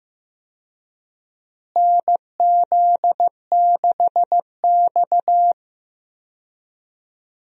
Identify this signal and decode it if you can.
Morse code, heard as a keyed tone, about 15 wpm, 700 Hz. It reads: NZ6X
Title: USA Call Signs - 4 Characters - Repeated 2x - 15wpm